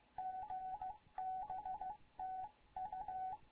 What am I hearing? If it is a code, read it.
Morse code, heard as a keyed tone, about 15 wpm, 750 Hz. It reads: GBTU
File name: Morze-HA2MN / hangp / 2 11 GYAKORLAT